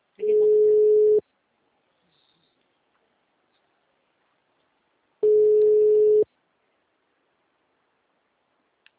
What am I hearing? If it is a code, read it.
Ringback tone (Europe, 425 Hz), with speech.
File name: out- -2003- - - .603